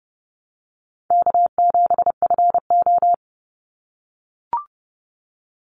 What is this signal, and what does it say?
Morse code, heard as a keyed tone, about 30 wpm, 700 Hz. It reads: K7FO
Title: NA SOTA Chaser Callsigns 30wpm